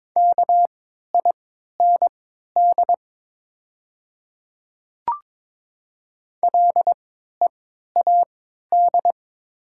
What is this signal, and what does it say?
Morse code, heard as a keyed tone, about 22 wpm, spaced as for 12 wpm, 700 Hz. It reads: KIND LEAD
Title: 3x Character Spacing - Top 500 Words - 22wpm